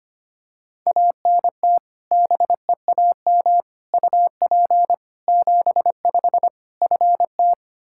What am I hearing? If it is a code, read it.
Morse code, heard as a keyed tone, about 25 wpm, 700 Hz. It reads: ANT BEAM UP 75 FT
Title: Practice QSO - Exchange 2 - Info - 25wpm